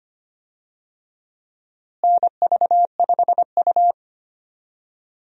Morse code, heard as a keyed tone, about 25 wpm, 700 Hz. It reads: NV5U
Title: USA Call Signs - 4 Characters - Repeated 3x - 25wpm